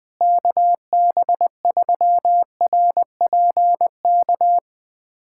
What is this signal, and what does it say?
Morse code, heard as a keyed tone, about 20 wpm, 700 Hz. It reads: KB3RPK
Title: USA Callsign DE Callsign 20wpm